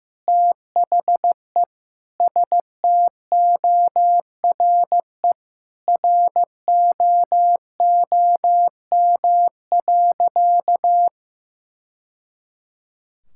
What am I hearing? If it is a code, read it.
Morse code, heard as a keyed tone, about 16 wpm, 700 Hz. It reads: THE STORE ROOM.
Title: Sentences from Top 1000 Words - Review Part 5 - 15wpm